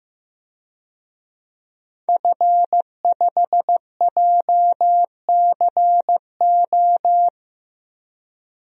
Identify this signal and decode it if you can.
Morse code, heard as a keyed tone, about 16 wpm, 700 Hz. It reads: F5JCO
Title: DXCC-100 Call Signs - 5 Characters - Repeated 3x - 15wpm